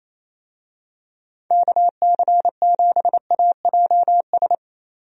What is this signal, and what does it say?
Morse code, heard as a keyed tone, about 27 wpm, 700 Hz. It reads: KC7AJS
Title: USA Call Signs - 6 Characters - 28wpm